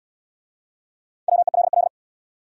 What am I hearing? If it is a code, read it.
Morse code, heard as a keyed tone, about 69 wpm, 700 Hz. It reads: WELL